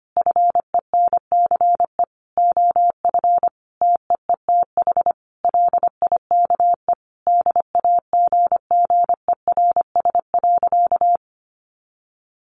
Morse code, heard as a keyed tone, about 25 wpm, 700 Hz. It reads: FENCE OF TEETH LIKE DAGGERS.
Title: A Sound Of Thunder 25wpm